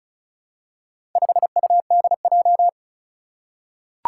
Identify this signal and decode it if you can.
Morse code, heard as a keyed tone, about 36 wpm, 700 Hz. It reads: 5UDJ
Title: Four Challenging Letter-Number Combinations 35wpm